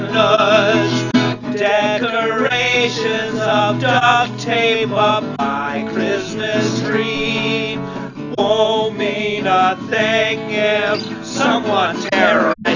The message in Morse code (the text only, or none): none